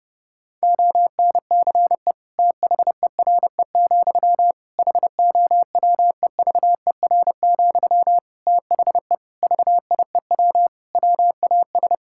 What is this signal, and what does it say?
Morse code, heard as a keyed tone, about 30 wpm, 700 Hz. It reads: ONCE THERE, HOWEVER, THE VIEW WAS
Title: Sentences from Top 1000 Words - Review Part 6 - 30wpm